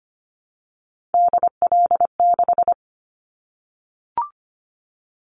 Morse code, heard as a keyed tone, about 25 wpm, 700 Hz. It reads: DL6
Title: Callsign Prefixes for Top 25 DXCC - 25wpm